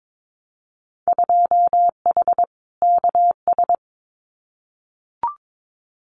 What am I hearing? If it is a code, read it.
Morse code, heard as a keyed tone, about 22 wpm, 700 Hz. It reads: 2H KS
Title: Winter Field Day Exchanges Encore 22wpm